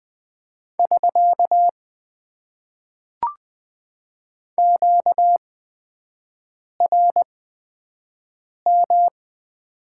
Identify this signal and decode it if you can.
Morse code, heard as a keyed tone, about 20 wpm, spaced as for 5 wpm, 700 Hz. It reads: <SK> QRM
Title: Top 90 QSO Elements - 8x Character Spacing - 20wpm